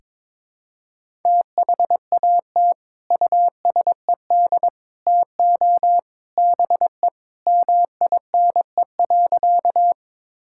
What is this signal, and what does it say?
Morse code, heard as a keyed tone, about 22 wpm, 700 Hz. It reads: THAT USED TO BE MINE.
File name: practice / 22wpm / Sentences with 3-Word Phrases - 22wpm